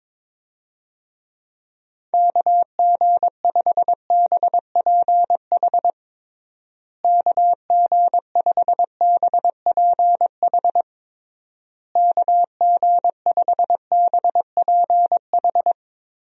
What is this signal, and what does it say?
Morse code, heard as a keyed tone, about 21 wpm, 700 Hz. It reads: KG5BPH KG5BPH KG5BPH
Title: USA Call Signs - 6 Characters - Repeated 3x - 22wpm